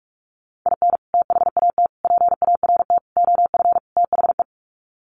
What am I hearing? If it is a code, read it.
Morse code, heard as a keyed tone, about 44 wpm, 700 Hz. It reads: IN THAT PART OF THE
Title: Sets of 5 Words Encore2 45wpm